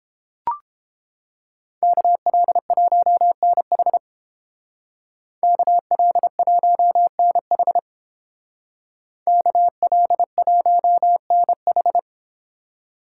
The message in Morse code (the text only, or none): KL1NH KL1NH KL1NH